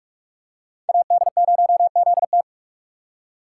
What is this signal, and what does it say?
Morse code, heard as a keyed tone, about 46 wpm, 700 Hz. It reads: AD0ZT
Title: USA Call Signs - 5 Characters - 6x Character Spacing - 45wpm